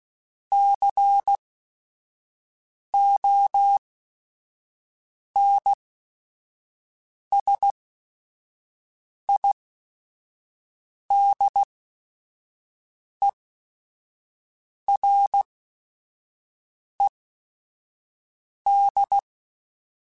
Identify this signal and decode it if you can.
Morse code, heard as a keyed tone, about 16 wpm, spaced as for 5 wpm, 800 Hz. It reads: CONSIDERED
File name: Technical Subjects / Technical Subjects-mp3s / XC @5